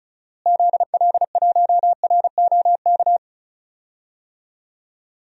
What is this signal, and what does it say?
Morse code, heard as a keyed tone, about 36 wpm, 700 Hz. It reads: ZL1ROK
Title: DXCC-100 Call Signs - 6 Characters - Repeated 3x - 35wpm